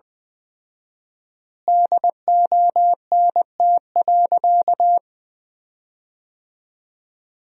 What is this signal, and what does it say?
Morse code, heard as a keyed tone, about 20 wpm, 700 Hz. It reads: DONT.
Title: 4x Word-Spacing - Sentences from Top 300 Words - 20wpm